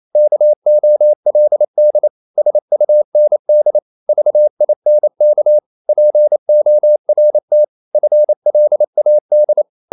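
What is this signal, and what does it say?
Morse code, heard as a keyed tone, about 28 wpm, 600 Hz. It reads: KOLD SUND VINK PORT FLAD
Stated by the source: Danish short words sent in Morse at 28 wpm, from CW med Gnister.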